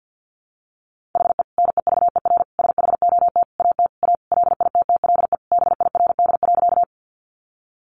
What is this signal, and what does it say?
Morse code, heard as a keyed tone, about 50 wpm, 700 Hz. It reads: HE NEVER SHOT AT A LITTLE BIRD.
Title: Sentences from Top 1100 Words 50wpm